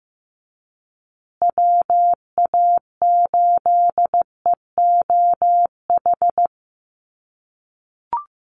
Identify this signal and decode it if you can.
Morse code, heard as a keyed tone, about 15 wpm, 700 Hz. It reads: WA8EOH
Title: USA Call Signs Repeated 2x 15wpm